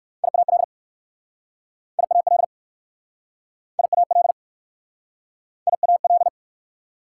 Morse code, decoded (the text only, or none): IAL IAL IAL IAL